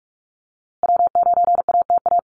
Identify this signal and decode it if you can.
Morse code, heard as a keyed tone, about 44 wpm, 700 Hz. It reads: W9ATA
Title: USA Call Signs Repeated 3x 45wpm